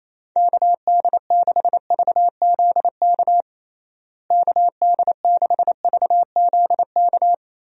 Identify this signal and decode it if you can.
Morse code, heard as a keyed tone, about 28 wpm, 700 Hz. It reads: KD6VZK KD6VZK
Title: USA Call Signs - 6 Characters - Repeated 3x - 28wpm